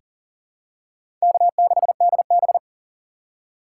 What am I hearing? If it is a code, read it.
Morse code, heard as a keyed tone, about 37 wpm, 700 Hz. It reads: K6DB
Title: USA Call Signs - 4 Characters - 4x Character Spacing - 40wpm